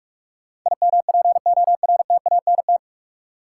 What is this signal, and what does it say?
Morse code, heard as a keyed tone, about 46 wpm, 700 Hz. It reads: IMPORTANT
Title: Sets of 3 Words - New Word Drill - 45wpm